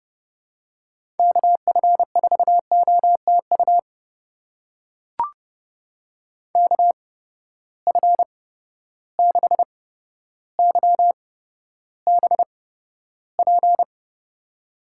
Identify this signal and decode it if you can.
Morse code, heard as a keyed tone, about 32 wpm, spaced as for 13 wpm, 700 Hz. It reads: KF4OTU KF6YBP
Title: USA Call Signs - 6 Characters - 8x Character Spacing - 30wpm